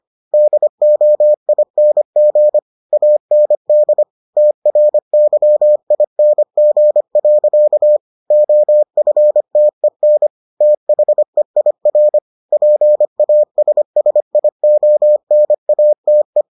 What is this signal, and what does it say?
Morse code, heard as a keyed tone, about 25 wpm, 600 Hz. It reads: DOING AND TRYING. OFTEN THEIR PASSIONATE